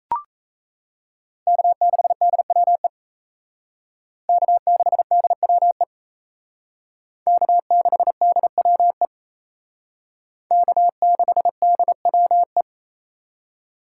Morse code, heard as a keyed tone, about 35 wpm, 700 Hz. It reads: K6DWE K6DWE K6DWE K6DWE
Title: Speed Racing - USA 1x3 Call Signs 28wpm